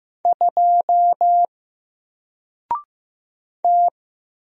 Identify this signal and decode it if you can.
Morse code, heard as a keyed tone, about 15 wpm, 700 Hz. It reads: 2 T
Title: Single Character (Letter:Number:Punctuation:Prosign) Alt 15wpm